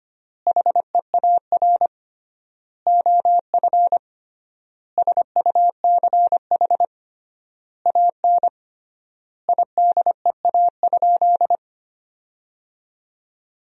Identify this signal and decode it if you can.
Morse code, heard as a keyed tone, about 25 wpm, 700 Hz. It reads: HEAR OF SUCH AN IDEA?
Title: 3x Word-Spacing - Sentences from Top 400 Words - 25wpm